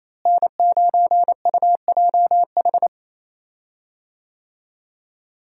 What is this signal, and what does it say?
Morse code, heard as a keyed tone, about 28 wpm, 700 Hz. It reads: N9UJH
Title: USA Call Signs - 5 Characters - Repeated 3x - 28wpm